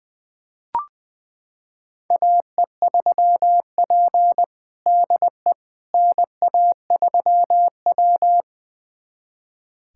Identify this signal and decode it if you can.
Morse code, heard as a keyed tone, about 20 wpm, 700 Hz. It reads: AE3P DE NA3W
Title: USA Callsign DE Callsign - 4 Characters - 20wpm